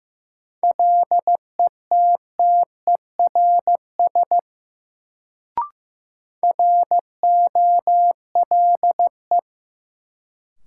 Morse code, heard as a keyed tone, about 15 wpm, 700 Hz. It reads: LETTERS ROLE
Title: Sentences from Top 1000 Words - Review Part 6 - New Word Drill - 15wpm